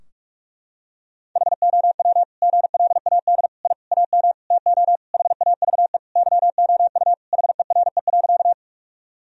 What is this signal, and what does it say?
Morse code, heard as a keyed tone, about 41 wpm, 700 Hz. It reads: HOW GLAD I AM TO HAVE YOU HERE.
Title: Sentences from Top 1000 Words - Review Part 2 - 45wpm